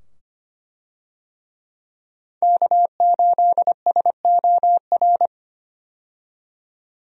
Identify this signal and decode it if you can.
Morse code, heard as a keyed tone, about 25 wpm, 700 Hz. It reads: K8SOR
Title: USA Call Signs - 5 Characters - 25wpm